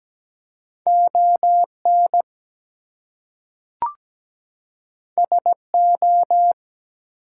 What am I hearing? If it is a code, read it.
Morse code, heard as a keyed tone, about 17 wpm, 700 Hz. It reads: ON SO
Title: Two Letter Words - 17wpm